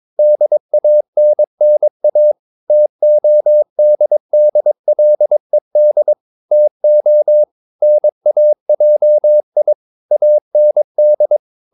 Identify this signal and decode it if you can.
Morse code, heard as a keyed tone, about 22 wpm, 600 Hz. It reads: DANNA TODDLED TO NAJI AND
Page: Romanen Never af Ken Follet på engelsk.